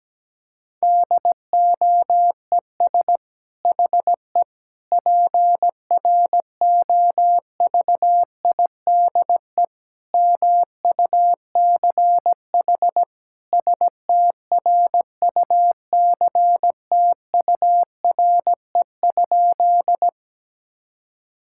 Morse code, heard as a keyed tone, about 17 wpm, 700 Hz. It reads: DOES HE PROVIDE MUCH STRUCTURE?
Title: Sentences from Top 1000 Words - Review Part 6 - 17wpm